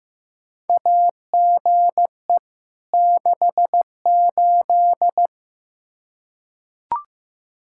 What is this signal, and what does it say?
Morse code, heard as a keyed tone, about 15 wpm, 700 Hz. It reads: AGE 68
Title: Practice QSO - Exchange 3 - More Info - 15wpm